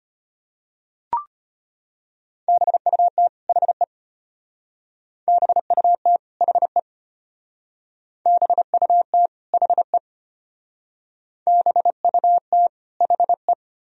Morse code, heard as a keyed tone, about 31 wpm, 700 Hz. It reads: BUT HE BUT HE BUT HE BUT HE
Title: Speed-Racing - Sets of 2 Words 25wpm